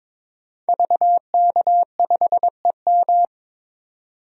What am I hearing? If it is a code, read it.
Morse code, heard as a keyed tone, about 22 wpm, 700 Hz. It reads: VK5EM
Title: DXCC-100 Call Signs - 5 Characters - 22wpm